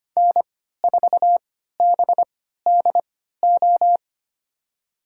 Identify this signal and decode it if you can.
Morse code, heard as a keyed tone, about 25 wpm, spaced as for 14 wpm, 700 Hz. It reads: N4BDO
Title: USA Call Signs - 5 Characters - 3x Character Spacing - 25wpm